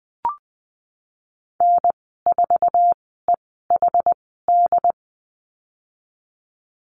Morse code, heard as a keyed tone, about 20 wpm, spaced as for 15 wpm, 700 Hz. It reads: N4EHD
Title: 076 Lesson 13g - Letter L - Call Signs 20wpm